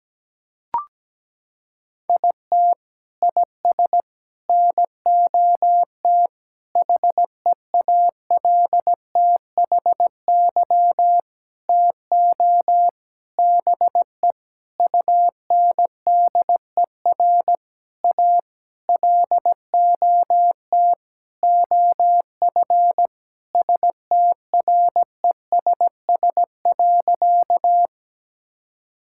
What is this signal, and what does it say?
Morse code, heard as a keyed tone, about 17 wpm, 700 Hz. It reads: IT IS NOT HEALTHY TO BE UNDER A LOT OF STRESS.
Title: Sentences from Top 1000 Words - Review Part 6 - 17wpm